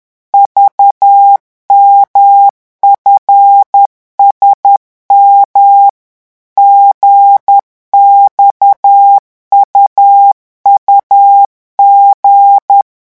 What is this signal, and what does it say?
Morse code, heard as a keyed tone, about 10 wpm, 800 Hz. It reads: VMFSM GXUUG